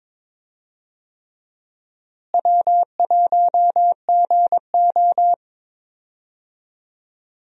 Morse code, heard as a keyed tone, about 22 wpm, 700 Hz. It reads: W1GO
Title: USA Call Signs - 4 Characters - Repeated 3x - 22wpm